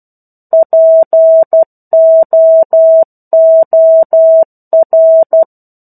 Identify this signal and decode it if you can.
Morse code, heard as a keyed tone, about 12 wpm, 650 Hz. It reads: POOR
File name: 01-Chapter-01 12 WPM